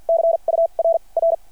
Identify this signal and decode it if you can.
Morse code, heard as a keyed tone, about 50 wpm, spaced as for 35 wpm, 660 Hz. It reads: CURR
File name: archived / CRSnet